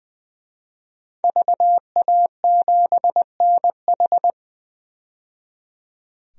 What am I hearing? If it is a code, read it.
Morse code, heard as a keyed tone, about 20 wpm, 700 Hz. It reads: VA7NH
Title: DXCC-100 Call Signs - 5 Characters - 20wpm